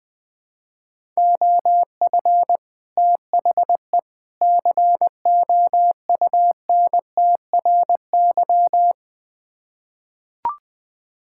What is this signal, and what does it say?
Morse code, heard as a keyed tone, about 20 wpm, 700 Hz. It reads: OF THE COUNTRY
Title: Build A Phrase - 3-Word Phrases - 20wpm